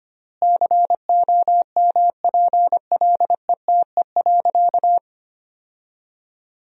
Morse code, heard as a keyed tone, about 25 wpm, 700 Hz. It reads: COMPLETE.
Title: 3x Word-Spacing - Sentences from Top 400 Words - 25wpm